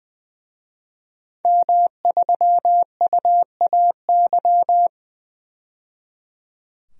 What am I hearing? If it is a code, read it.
Morse code, heard as a keyed tone, about 20 wpm, 700 Hz. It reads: M3UAY
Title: DXCC-100 Call Signs - 5 Characters - 20wpm